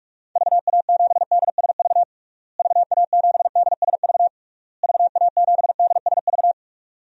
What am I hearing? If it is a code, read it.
Morse code, heard as a keyed tone, about 46 wpm, 700 Hz. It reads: VA7DSV VA7DSV VA7DSV
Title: DXCC-100 Call Signs - 6 Characters - Repeated 3x - 45wpm